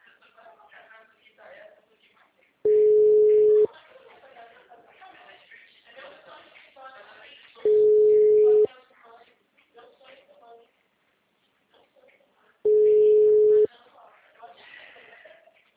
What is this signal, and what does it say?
Ringback tone (Europe, 425 Hz), with speech.